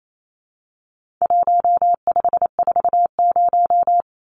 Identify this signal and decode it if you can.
Morse code, heard as a keyed tone, about 28 wpm, 700 Hz. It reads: 1540
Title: Numbers - 4-Digit 28wpm